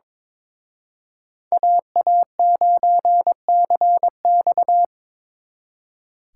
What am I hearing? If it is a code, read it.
Morse code, heard as a keyed tone, about 26 wpm, 700 Hz. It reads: AA9CX
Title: Speed Racing - USA 2x2 Call Signs 22wpm